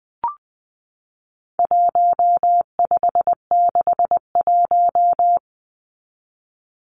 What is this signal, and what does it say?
Morse code, heard as a keyed tone, about 20 wpm, 700 Hz. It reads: 1561
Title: Numbers - 4-Digit 20wpm